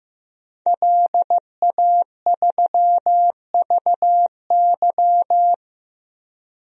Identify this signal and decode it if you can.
Morse code, heard as a keyed tone, about 16 wpm, 700 Hz. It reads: LA3VY
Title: DXCC-100 Call Signs - 5 Characters - 15wpm